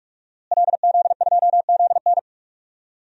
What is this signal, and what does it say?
Morse code, heard as a keyed tone, about 41 wpm, 700 Hz. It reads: LZ2ZN
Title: DXCC-100 Call Signs - 5 Characters - Repeated 3x - 45wpm